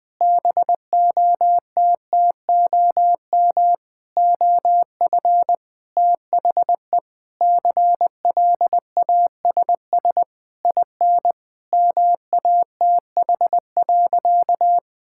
Morse code, heard as a keyed tone, about 20 wpm, 700 Hz. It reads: BOTTOM OF THE CLASS IN MATH.
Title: Sentences from Top 1000 Words - Review Part 6 - 20wpm